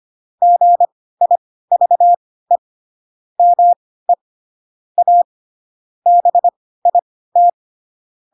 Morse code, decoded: GIVE ME A BIT

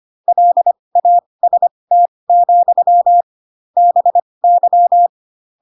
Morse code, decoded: LAST, BY

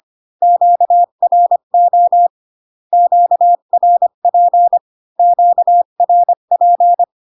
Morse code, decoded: QRO QRP QRP

